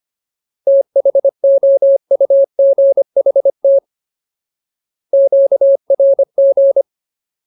Morse code, heard as a keyed tone, about 25 wpm, 550 Hz. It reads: THOUGHT QRG
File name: Ultimate Word List / 25WPM Ultimate